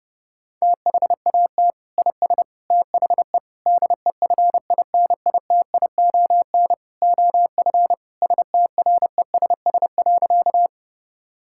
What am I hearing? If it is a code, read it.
Morse code, heard as a keyed tone, about 30 wpm, 700 Hz. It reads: THAT IS THE DEFINITION OF STRESS.
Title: Sentences from Top 1000 Words - Review Part 6 - 30wpm